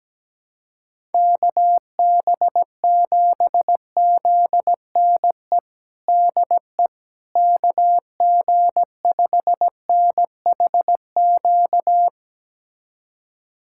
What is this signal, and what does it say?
Morse code, heard as a keyed tone, about 17 wpm, 700 Hz. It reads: KB7ZNE DE KG5NHQ
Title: USA Callsign DE Callsign 17wpm